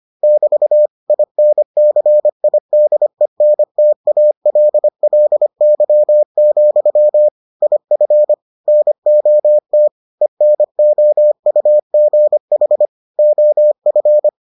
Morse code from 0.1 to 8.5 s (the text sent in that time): <BT> INCIDENTALLY, IF